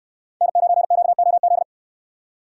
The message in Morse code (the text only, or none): N9QOZ